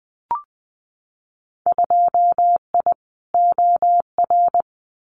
Morse code, heard as a keyed tone, about 20 wpm, 700 Hz. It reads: 2I OR
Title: Winter Field Day Exchanges Encore 20wpm